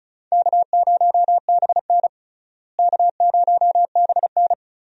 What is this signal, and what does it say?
Morse code, heard as a keyed tone, about 34 wpm, 700 Hz. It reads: K0BN K0BN
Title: USA Call Signs - 4 Characters - Repeated 3x - 35wpm